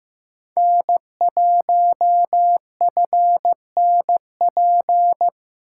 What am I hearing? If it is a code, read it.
Morse code, heard as a keyed tone, about 15 wpm, 700 Hz. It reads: N1FNP
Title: USA Callsign DE Callsign - 5 Characters - 15wpm